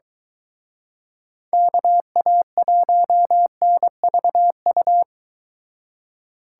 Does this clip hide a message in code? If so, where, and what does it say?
Yes: Morse code from 1.5 to 5.0 s, heard as a keyed tone, about 24 wpm, 700 Hz. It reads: KA1NVU